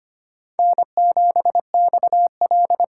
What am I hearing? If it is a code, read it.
Morse code, heard as a keyed tone, about 25 wpm, 700 Hz. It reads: N7XL
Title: USA Call Signs - 5 Characters - 8x Character Spacing - 25wpm